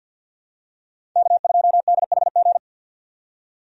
Morse code, heard as a keyed tone, about 48 wpm, 700 Hz. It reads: K2DHG
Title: USA Call Signs - 5 Characters - 50wpm